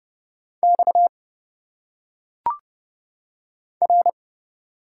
Morse code, heard as a keyed tone, about 29 wpm, 700 Hz. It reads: X R
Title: Single Letters - Phonetic Pronunciation 30wpm